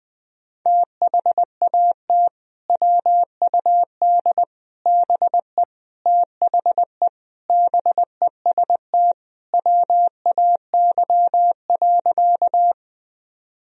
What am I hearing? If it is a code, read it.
Morse code, heard as a keyed tone, about 21 wpm, 700 Hz. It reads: THAT WUD BE THE BEST WAY.
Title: Sentences from Top 400 Words and Abbreviations 20wpm